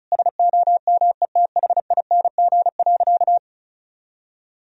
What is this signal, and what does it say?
Morse code, heard as a keyed tone, about 36 wpm, 700 Hz. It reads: SOMETHING.
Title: Sentences from Top 1000 Words - Review Part 4 - 35wpm